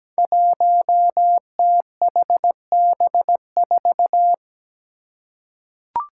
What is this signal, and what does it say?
Morse code, heard as a keyed tone, about 17 wpm, 700 Hz. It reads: 1THB4
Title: Five Challenging Letter-Number Combinations 17wpm